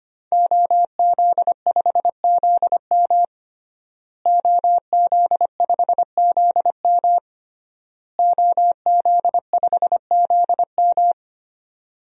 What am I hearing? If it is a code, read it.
Morse code, heard as a keyed tone, about 25 wpm, 700 Hz. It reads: OZ5ZM OZ5ZM OZ5ZM
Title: DXCC-100 Call Signs - 5 Characters - Repeated 3x - 25wpm